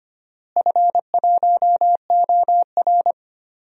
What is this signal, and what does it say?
Morse code, heard as a keyed tone, about 25 wpm, 700 Hz. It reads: F1OR
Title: DXCC-100 Call Signs - 4 Characters - Repeated 2x - 25wpm